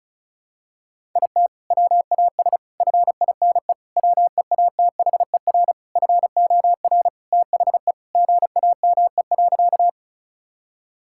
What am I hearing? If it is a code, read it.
Morse code, heard as a keyed tone, about 36 wpm, 690 Hz. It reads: IT WAS FINE WEATHER FOR THE GAME.
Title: Sentences from Top 1000 Words - Review Part 5 - 35wpm